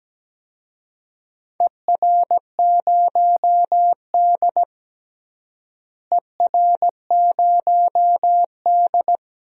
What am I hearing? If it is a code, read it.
Morse code, heard as a keyed tone, about 17 wpm, 700 Hz. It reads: ER0D ER0D
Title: DXCC-100 Call Signs - 4 Characters - Repeated 3x - 17wpm